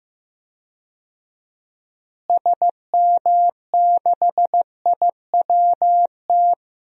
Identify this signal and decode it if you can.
Morse code, heard as a keyed tone, about 16 wpm, 700 Hz. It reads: SM6IWT
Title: DXCC-100 Call Signs - 6 Characters - Repeated 2x - 15wpm